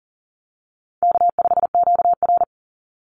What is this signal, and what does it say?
Morse code, heard as a keyed tone, about 40 wpm, 700 Hz. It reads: K5QR
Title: NA SOTA Chaser Callsigns 40wpm